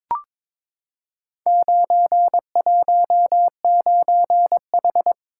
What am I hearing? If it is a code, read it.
Morse code, heard as a keyed tone, about 22 wpm, 700 Hz. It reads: 919H
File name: practice / 22wpm / Four Letter-Number Combinations - 22wpm